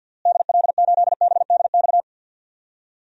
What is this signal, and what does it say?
Morse code, heard as a keyed tone, about 46 wpm, 700 Hz. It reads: DL8BDX
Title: DXCC-100 Call Signs - 6 Characters - Repeated 3x - 50wpm